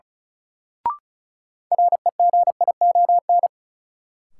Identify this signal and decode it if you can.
Morse code, heard as a keyed tone, about 36 wpm, 700 Hz. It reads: REGION
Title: Sentences from Top 1500 Words - New Word Drill - 35wpm